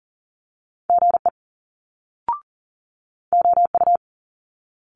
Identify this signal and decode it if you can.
Morse code, heard as a keyed tone, about 41 wpm, 700 Hz. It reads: GE OU
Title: Multi-letter Phonemes Quick Fire 40wpm